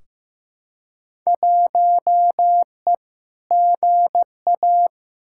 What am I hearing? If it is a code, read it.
Morse code, heard as a keyed tone, about 15 wpm, 700 Hz. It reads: 1E GA
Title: Field Day Exchanges - 15wpm